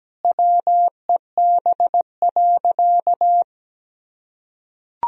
Morse code, heard as a keyed tone, about 17 wpm, 700 Hz. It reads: WEB.
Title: Sentences from Top 1000 Words - Review Part 4 - 17wpm